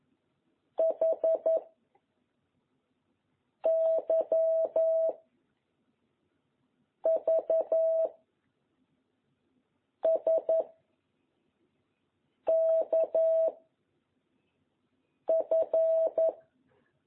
Morse code, heard as a keyed tone, about 11 wpm, 660 Hz. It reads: H Y V S K F
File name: jh8cbh / 2021cw